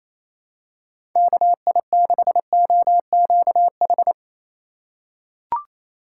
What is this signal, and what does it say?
Morse code, heard as a keyed tone, about 28 wpm, 700 Hz. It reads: KI6OQH
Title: USA Call Signs - 2x Character Spacing - 28wpm